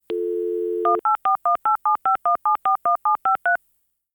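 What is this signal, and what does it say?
DTMF, heard as telephone touch tones, tones of about 120 ms. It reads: 18418751741753
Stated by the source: DTMF modem dialing sound.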